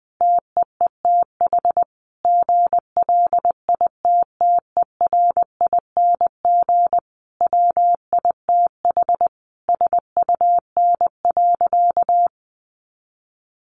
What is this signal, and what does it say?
Morse code, heard as a keyed tone, about 20 wpm, 700 Hz. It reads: TEETH GLITTERING WITH SUN.
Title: A Sound Of Thunder 20wpm